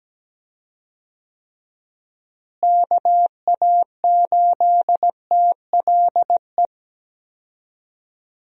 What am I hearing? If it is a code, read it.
Morse code, heard as a keyed tone, about 17 wpm, 700 Hz. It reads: KA8TLE
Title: USA Call Signs - 6 Characters - Repeated 2x - 17wpm